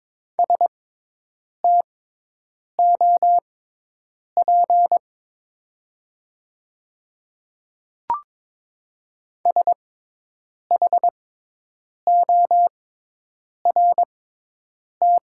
Morse code, heard as a keyed tone, about 22 wpm, spaced as for 8 wpm, 700 Hz. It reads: STOP SHORT
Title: 6x Character Spacing - Top 500 Words - 22wpm